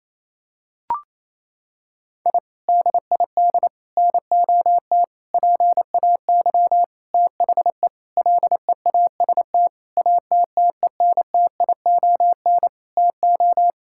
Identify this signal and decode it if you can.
Morse code, heard as a keyed tone, about 28 wpm, 700 Hz. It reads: I DID NOT PAY THE LEAST ATTENTION TO
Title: Sentences from Top 1000 Words - Review Part 6 - 28wpm